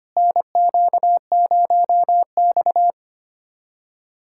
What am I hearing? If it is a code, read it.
Morse code, heard as a keyed tone, about 25 wpm, 700 Hz. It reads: NQ0X
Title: USA Callsign DE Callsign 25wpm